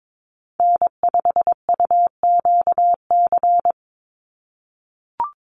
Morse code, heard as a keyed tone, about 22 wpm, 700 Hz. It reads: N5UQC
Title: USA Call Signs 22wpm